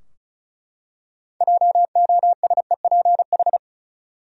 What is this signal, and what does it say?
Morse code, heard as a keyed tone, about 34 wpm, 700 Hz. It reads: JOSEPH
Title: USA Common Names - 35wpm